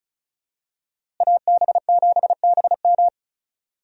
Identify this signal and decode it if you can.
Morse code, heard as a keyed tone, about 36 wpm, 700 Hz. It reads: AB7BM
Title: USA Call Signs - 5 Characters - Repeated 3x - 35wpm